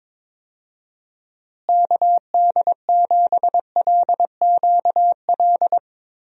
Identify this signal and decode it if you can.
Morse code, heard as a keyed tone, about 22 wpm, 700 Hz. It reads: KD7LQL
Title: USA Call Signs - 6 Characters - Repeated 2x - 22wpm